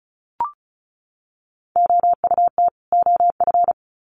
Morse code, heard as a keyed tone, about 34 wpm, 700 Hz. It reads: OUT OF
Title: Sets of 2 Words 35wpm